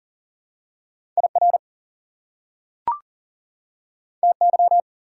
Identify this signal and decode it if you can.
Morse code, heard as a keyed tone, about 37 wpm, 700 Hz. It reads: IR TY
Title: Prefixes and Suffixes 40wpm